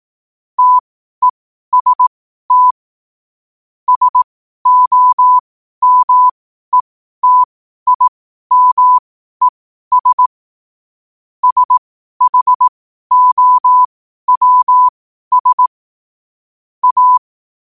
Morse code, heard as a keyed tone, about 18 wpm, spaced as for 12 wpm, 1000 Hz. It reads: TEST SOMETIMES SHOWS A
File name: qotdfiles / 12wpm